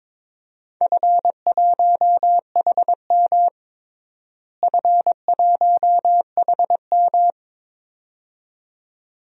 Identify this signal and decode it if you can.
Morse code, heard as a keyed tone, about 22 wpm, 700 Hz. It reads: F1HM F1HM
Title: DXCC-100 Call Signs - 4 Characters - Repeated 2x - 22wpm